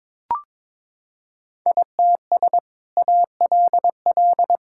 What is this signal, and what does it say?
Morse code, heard as a keyed tone, about 22 wpm, 700 Hz. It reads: ITS ALL
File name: practice / 22wpm / Sentences with 2-Word Phrases - 22wpm